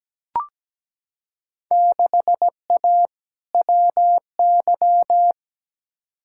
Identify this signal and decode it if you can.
Morse code, heard as a keyed tone, about 17 wpm, 700 Hz. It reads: 6A WY
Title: Field Day Exchanges Encore - 17wpm